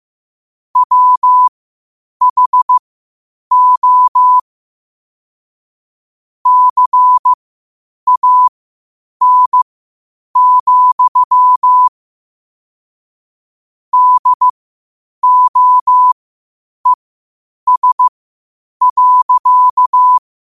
Morse code, heard as a keyed tone, about 15 wpm, spaced as for 8 wpm, 1000 Hz. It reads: WHO CAN, DOES.
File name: qotdfiles / 8wpm